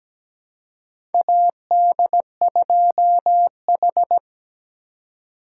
Morse code, heard as a keyed tone, about 17 wpm, 700 Hz. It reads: AD2H
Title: USA Call Signs - 4 Characters - 4x Character Spacing - 17wpm